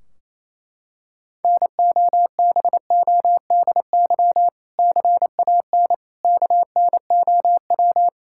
Morse code, heard as a keyed tone, about 27 wpm, 700 Hz. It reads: NOBODY CAN KNOW
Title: Sentences from Top 1600 Words 28wpm